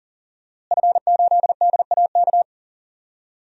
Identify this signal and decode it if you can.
Morse code, heard as a keyed tone, about 40 wpm, 700 Hz. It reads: F8DAK
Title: DXCC-100 Call Signs - 5 Characters - Repeated 3x - 40wpm